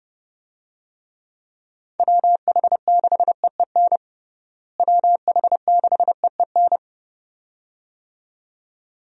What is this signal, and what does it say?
Morse code, heard as a keyed tone, about 30 wpm, 700 Hz. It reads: WH6EEN WH6EEN
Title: USA Call Signs - 6 Characters - Repeated 2x - 30wpm